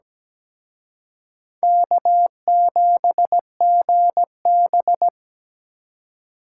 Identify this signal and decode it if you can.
Morse code, heard as a keyed tone, about 20 wpm, 700 Hz. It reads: K7GB
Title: Speed Racing - USA 1x2 Call Signs 17wpm